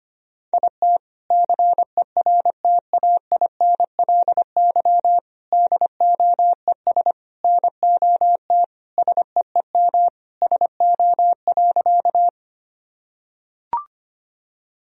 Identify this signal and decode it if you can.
Morse code, heard as a keyed tone, about 25 wpm, 700 Hz. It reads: IT CERTAINLY DOES NOT SEEM SO.
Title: Sentences from Top 1000 Words - Review Part 1 - 25wpm